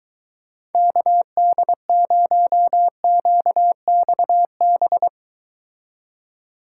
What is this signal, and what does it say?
Morse code, heard as a keyed tone, about 23 wpm, 700 Hz. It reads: KD0QXB